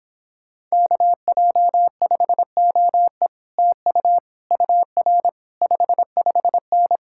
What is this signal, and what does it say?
Morse code, heard as a keyed tone, about 26 wpm, 700 Hz. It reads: KJ5OE TU UR 55N